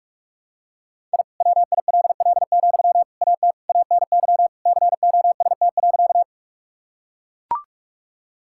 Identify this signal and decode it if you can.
Morse code, heard as a keyed tone, about 41 wpm, 700 Hz. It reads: I WILL, AT ANY COST.
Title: Sentences from Top 1000 Words - Review Part 5 - 45wpm